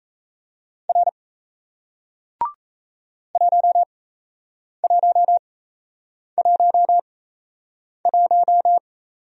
Morse code, heard as a keyed tone, about 37 wpm, spaced as for 8 wpm, 700 Hz. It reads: R 1111